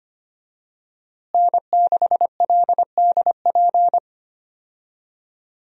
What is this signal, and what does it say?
Morse code, heard as a keyed tone, about 25 wpm, 700 Hz. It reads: N6LDP
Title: USA Call Signs - 5 Characters - Repeated 2x - 25wpm